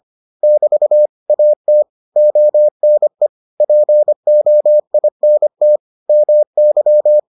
Morse code, heard as a keyed tone, about 25 wpm, 600 Hz. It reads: <BT> AT ONE POINT MY